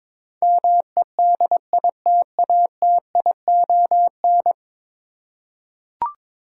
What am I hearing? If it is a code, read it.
Morse code, heard as a keyed tone, about 22 wpm, 700 Hz. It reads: MEDITATION
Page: morse-code-ninja-practice-22wpm